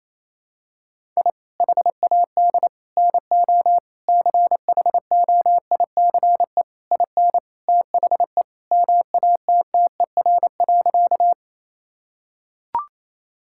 Morse code, heard as a keyed tone, about 28 wpm, 700 Hz. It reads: I HAD NO CHOICE IN THE MATTER.
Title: Sentences from Top 1000 Words - Review Part 6 - 28wpm